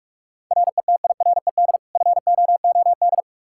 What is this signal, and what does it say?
Morse code, heard as a keyed tone, about 46 wpm, 700 Hz. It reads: RETIRED FOOD
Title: Practice QSO - Exchange 3 - More Info - 45wpm